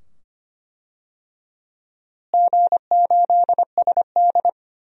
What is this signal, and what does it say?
Morse code, heard as a keyed tone, about 25 wpm, 700 Hz. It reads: G8SD
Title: DXCC-100 Call Signs - 4 Characters - Repeated 2x - 25wpm